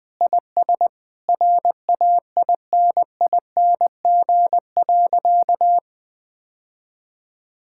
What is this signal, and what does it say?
Morse code, heard as a keyed tone, about 20 wpm, 700 Hz. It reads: IS RAINING.
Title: Sentences with 8-Word Phrases - 20wpm